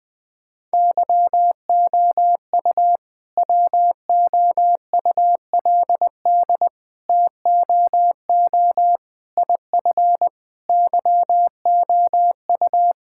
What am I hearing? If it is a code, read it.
Morse code, heard as a keyed tone, about 20 wpm, 700 Hz. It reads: YOU WOULD TOO IF YOU
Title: All-Age Appropriate Jokes - 20wpm